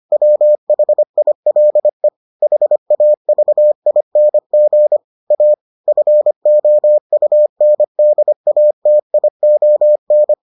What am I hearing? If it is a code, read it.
Morse code, heard as a keyed tone, about 25 wpm, 600 Hz. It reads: WHILE HAVING A FOUNDATION